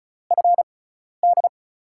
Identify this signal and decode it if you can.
Morse code, heard as a keyed tone, about 36 wpm, spaced as for 12 wpm, 700 Hz. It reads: FD
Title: USA Call Signs - 6 Characters - 6x Character Spacing - 35wpm